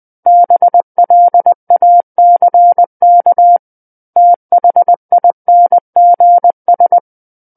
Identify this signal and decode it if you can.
Morse code, heard as a keyed tone, about 20 wpm, 700 Hz. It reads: BLACK THINGS
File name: 20 WPM